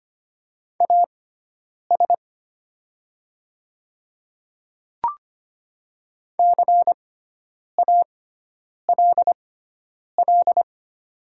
Morse code, heard as a keyed tone, about 25 wpm, spaced as for 9 wpm, 690 Hz. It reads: AS CALL
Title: 6x Character Spacing - Top 200 Words - 25wpm